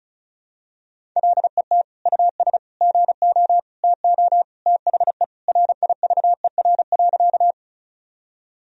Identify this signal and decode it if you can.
Morse code, heard as a keyed tone, about 36 wpm, 700 Hz. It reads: LET US GO TO THE RIVER.
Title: Sentences from Top 1000 Words - Review Part 1 - 35wpm